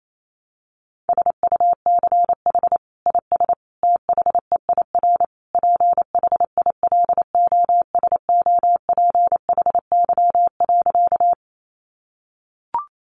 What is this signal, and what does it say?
Morse code, heard as a keyed tone, about 28 wpm, 700 Hz. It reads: SUCH IS THEIR PHILOSOPHY.
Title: Sentences from Top 900 Words 28wpm